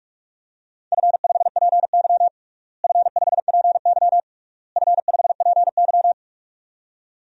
Morse code, heard as a keyed tone, about 41 wpm, 700 Hz. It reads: F5PY F5PY F5PY
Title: DXCC-100 Call Signs - 4 Characters - Repeated 3x - 45wpm